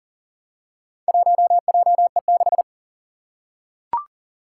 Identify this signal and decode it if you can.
Morse code, heard as a keyed tone, about 40 wpm, 700 Hz. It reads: 1JE6
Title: Four Challenging Letter-Number Combinations 40wpm